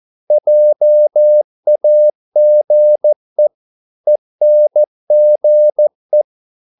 Morse code, heard as a keyed tone, about 14 wpm, 600 Hz. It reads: JAGE ENGE
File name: Korte ord DK 14wpm